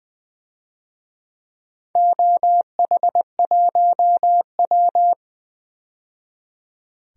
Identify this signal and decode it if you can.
Morse code, heard as a keyed tone, about 20 wpm, 700 Hz. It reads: OH1W
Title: DXCC-100 Call Signs - 4 Characters - 20wpm